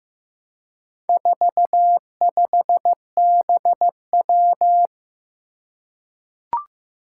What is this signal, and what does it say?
Morse code, heard as a keyed tone, about 15 wpm, 700 Hz. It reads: 45BW
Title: Four Challenging Letter-Number Combinations 15wpm